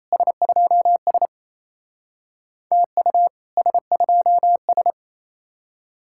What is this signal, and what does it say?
Morse code, heard as a keyed tone, about 36 wpm, 700 Hz. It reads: S2S TU S2S